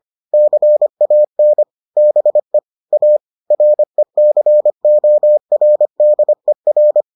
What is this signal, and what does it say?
Morse code, heard as a keyed tone, about 25 wpm, 600 Hz. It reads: CAN BE A RECORDER